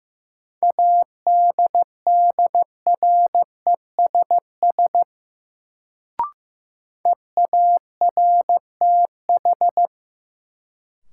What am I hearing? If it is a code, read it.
Morse code, heard as a keyed tone, about 15 wpm, 700 Hz. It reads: ADDRESS EARTH
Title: Sets of 5 Words Encore - New Word Drill - 15wpm